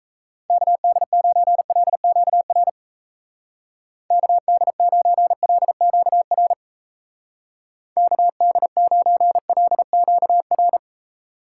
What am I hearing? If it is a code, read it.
Morse code, heard as a keyed tone, about 35 wpm, 700 Hz. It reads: KD9LQR KD9LQR KD9LQR